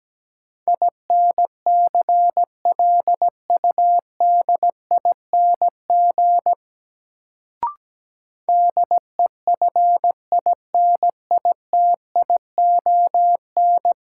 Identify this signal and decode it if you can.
Morse code, heard as a keyed tone, about 17 wpm, 700 Hz. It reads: INCLUDING DEFINITION
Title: Sentences from Top 1000 Words - Review Part 4 - New Word Drill - 17wpm